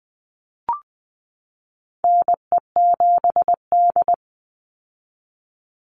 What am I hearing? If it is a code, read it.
Morse code, heard as a keyed tone, about 20 wpm, 700 Hz. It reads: NE7D
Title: Call Signs - 20wpm